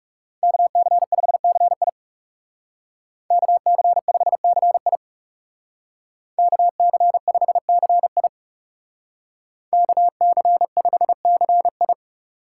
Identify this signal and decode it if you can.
Morse code, heard as a keyed tone, about 37 wpm, 700 Hz. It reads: KC5CI KC5CI KC5CI KC5CI
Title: Speed Racing - USA 2x2 Call Signs 30wpm